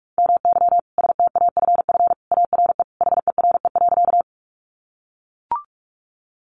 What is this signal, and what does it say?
Morse code, heard as a keyed tone, about 44 wpm, 700 Hz. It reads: MY STAFF ARE HERE.
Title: Sentences from Top 1200 Words 45wpm